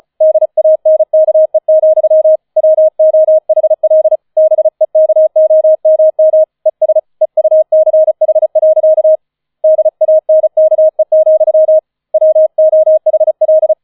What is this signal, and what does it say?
Morse code, heard as a keyed tone, about 35 wpm, 630 Hz. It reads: DANKE, WOHL BEKOMM ES EUCH. DANKE, WOHL
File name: 005 Fünfter Streich 35wpm